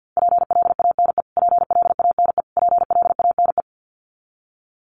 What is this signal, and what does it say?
Morse code, heard as a keyed tone, about 50 wpm, 700 Hz. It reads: PLANE PLANE PLANE
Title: Top 500 Words Repeated 50wpm